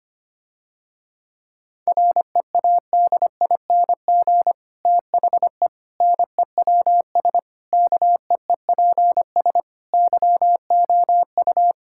Morse code, heard as a keyed tone, about 25 wpm, 700 Hz. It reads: READING THE NEWS KEEPS YOU
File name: practice / 25wpm / Difficult Sentences with Hints - 25wpm